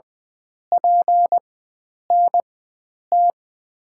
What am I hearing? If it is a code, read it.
Morse code, heard as a keyed tone, about 20 wpm, spaced as for 9 wpm, 700 Hz. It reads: PNT
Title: USA Call Signs - 6 Characters - 4x Character Spacing - 20wpm